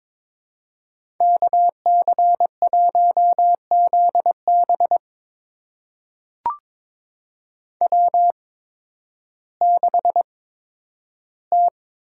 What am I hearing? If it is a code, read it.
Morse code, heard as a keyed tone, about 21 wpm, spaced as for 12 wpm, 700 Hz. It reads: KC1ZB W6T
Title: USA Call Signs - 5 Characters - 8x Character Spacing - 22wpm